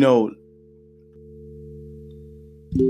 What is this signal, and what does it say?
Dial tone (North America), 350 + 440 Hz.